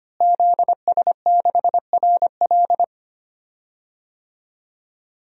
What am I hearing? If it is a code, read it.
Morse code, heard as a keyed tone, about 25 wpm, 700 Hz. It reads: ZS6RL
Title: DXCC-100 Call Signs - 5 Characters - Repeated 3x - 25wpm